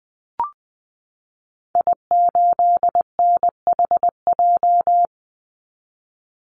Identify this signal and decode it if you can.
Morse code, heard as a keyed tone, about 20 wpm, 700 Hz. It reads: I8NHJ
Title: Call Signs - 20wpm